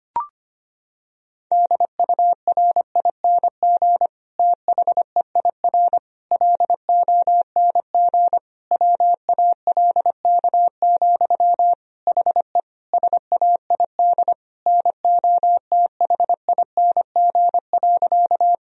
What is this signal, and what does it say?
Morse code, heard as a keyed tone, about 25 wpm, 700 Hz. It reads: DURING THEIR LONG WALK, HE SAID NOTHING.
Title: Sentences from Top 1000 Words - Review Part 2 - 25wpm